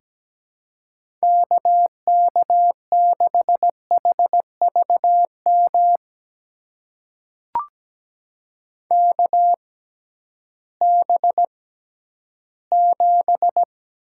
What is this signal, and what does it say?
Morse code, heard as a keyed tone, about 17 wpm, spaced as for 11 wpm, 700 Hz. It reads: KK6HVM KB7
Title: USA Call Signs - 6 Characters - 6x Character Spacing - 17wpm